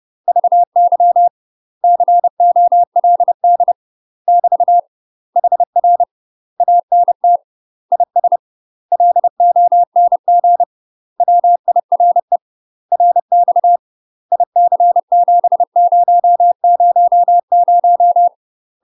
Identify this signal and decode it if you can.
Morse code, heard as a keyed tone, about 30 wpm, 700 Hz. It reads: VY COLD <BT> HR ANT IS LONG WIRE RX IC7000
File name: UK MASTER QSOs TRAINING / 30WPM / UK MASTER QSOs 0031